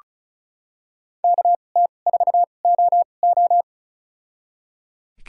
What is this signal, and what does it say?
Morse code, heard as a keyed tone, about 36 wpm, spaced as for 25 wpm, 700 Hz. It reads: KT4OO
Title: USA Call Signs - 2x Character Spacing - 35wpm